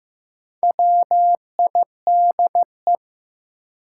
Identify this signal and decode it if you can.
Morse code, heard as a keyed tone, about 16 wpm, 700 Hz. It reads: WIDE
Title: Sentences from Top 900 Words - New Word Drill - 15wpm